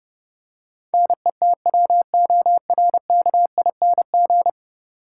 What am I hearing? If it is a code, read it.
Morse code, heard as a keyed tone, about 32 wpm, 700 Hz. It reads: NETWORKING
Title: Practice QSO - Exchange 3 - More Info - 30wpm